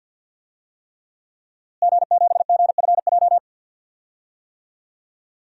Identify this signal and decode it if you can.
Morse code, heard as a keyed tone, about 47 wpm, 700 Hz. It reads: G7GFJ
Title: DXCC-100 Call Signs - 5 Characters - Repeated 3x - 50wpm